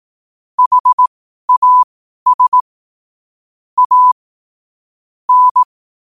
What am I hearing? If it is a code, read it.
Morse code, heard as a keyed tone, about 18 wpm, spaced as for 12 wpm, 1000 Hz. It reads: HAS A N
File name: qotdfiles / 12wpm